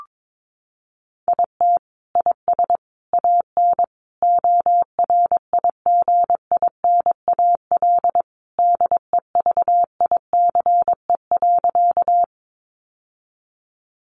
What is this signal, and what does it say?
Morse code, heard as a keyed tone, about 22 wpm, 700 Hz. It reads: IT IS AN ORIGINAL DEVICE.
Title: Sentences from Top 1500 Words 22wpm